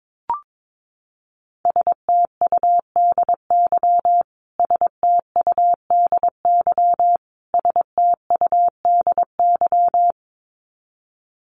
Morse code, heard as a keyed tone, about 22 wpm, 700 Hz. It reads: STUDY STUDY STUDY
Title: Top 500 Words Repeated 22wpm